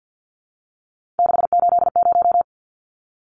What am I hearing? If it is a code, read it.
Morse code, heard as a keyed tone, about 50 wpm, 700 Hz. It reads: 680
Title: Numbers - 3-Digit 50wpm